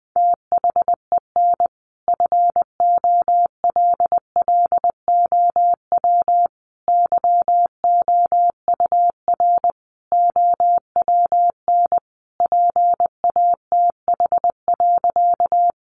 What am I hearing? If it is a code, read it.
Morse code, heard as a keyed tone, about 20 wpm, 700 Hz. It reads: THEN FOLLOW YOUR OWN PATH.
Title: Sentences from Top 1400 Words 20wpm